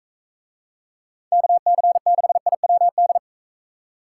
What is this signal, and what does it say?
Morse code, heard as a keyed tone, about 45 wpm, 700 Hz. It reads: KC6IWD